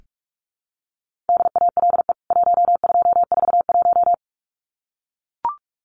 Morse code, heard as a keyed tone, about 44 wpm, 700 Hz. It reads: DALE 1241
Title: CWT Exchange 45wpm